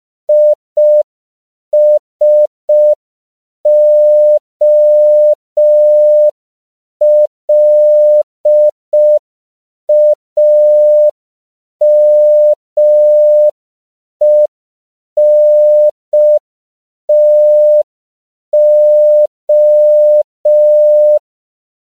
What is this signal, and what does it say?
Morse code, heard as a keyed tone, about 5 wpm, 600 Hz. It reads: ISOLAMENTO
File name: CW / 5PPM